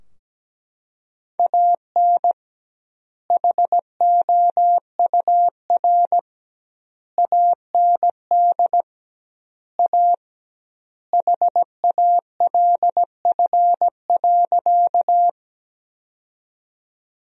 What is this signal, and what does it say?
Morse code, heard as a keyed tone, about 17 wpm, 700 Hz. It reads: AN HOUR AND A HALF.
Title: 2x Word-Spacing - Sentences from Top 400 Words - 17wpm